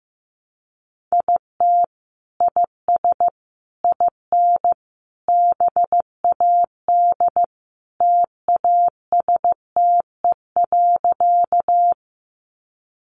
Morse code, heard as a keyed tone, about 15 wpm, 700 Hz. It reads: IT IS IN BAD TASTE.
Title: Sentences from Top 1100 Words 15wpm